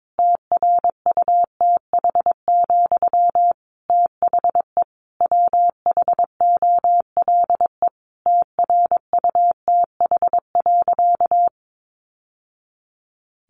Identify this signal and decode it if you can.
Morse code, heard as a keyed tone, about 22 wpm, 700 Hz. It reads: TRUTH, THE WHOLE TRUTH.
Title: Sentences from Top 800 Words 22wpm